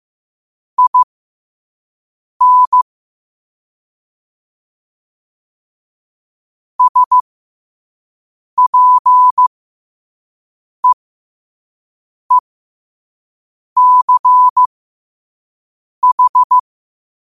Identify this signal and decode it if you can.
Morse code, heard as a keyed tone, about 15 wpm, spaced as for 5 wpm, 1000 Hz. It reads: IN SPEECH
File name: qotdfiles / 5wpm